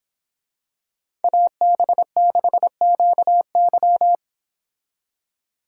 Morse code, heard as a keyed tone, about 26 wpm, 700 Hz. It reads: AB6QY
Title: Speed Racing - USA Call Signs 22wpm